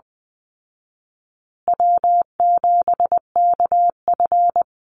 Morse code, heard as a keyed tone, about 20 wpm, 700 Hz. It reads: W7KF
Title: Call Signs Repeated 3x - 20wpm